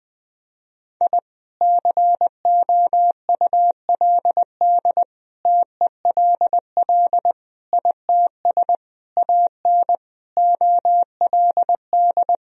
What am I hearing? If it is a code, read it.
Morse code, heard as a keyed tone, about 20 wpm, 700 Hz. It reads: I COULD TELL ITS AN OLD
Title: Sentences from Top 1000 Words - Review Part 2 - 20wpm